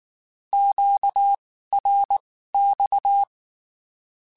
Morse code, heard as a keyed tone, about 19 wpm, spaced as for 14 wpm, 800 Hz. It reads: QRX
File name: (spaced) / QSOs Spaced-mp3s / kc7sac @18 sp